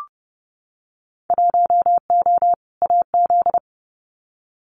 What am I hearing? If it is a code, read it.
Morse code, heard as a keyed tone, about 30 wpm, 700 Hz. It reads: 1O AZ
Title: Winter Field Day Exchanges 30wpm